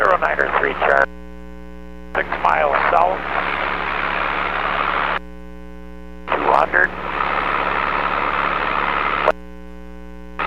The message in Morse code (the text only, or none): none